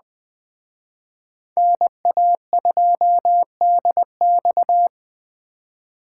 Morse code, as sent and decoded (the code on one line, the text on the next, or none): -. .- ..--- -.. -..-
NA2DX